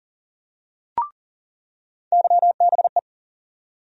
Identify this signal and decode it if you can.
Morse code, heard as a keyed tone, about 40 wpm, 700 Hz. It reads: YBE
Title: USA Call Sign Postfixes - 3 Letters 40wpm